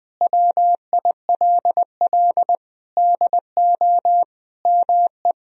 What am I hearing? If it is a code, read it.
Morse code, heard as a keyed tone, about 20 wpm, 700 Hz. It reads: WILL DO ME
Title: Sentences from Top 1000 Words - Review Part 1 - 20wpm